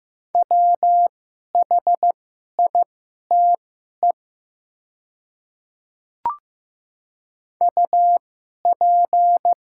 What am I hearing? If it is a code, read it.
Morse code, heard as a keyed tone, about 15 wpm, spaced as for 11 wpm, 700 Hz. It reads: WHITE UP
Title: 2x Character Spacing - Top 300 Words - 15wpm